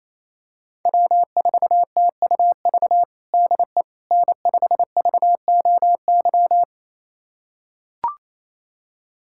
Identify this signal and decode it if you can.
Morse code, heard as a keyed tone, about 27 wpm, 700 Hz. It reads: W4TUV DE N5VOY
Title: USA Callsign DE Callsign - 5 Characters - 28wpm